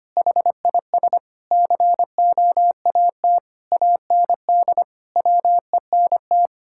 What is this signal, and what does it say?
Morse code, heard as a keyed tone, about 25 wpm, 700 Hz. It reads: HIS COAT AND WENT
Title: Sentences from Top 1600 Words 25wpm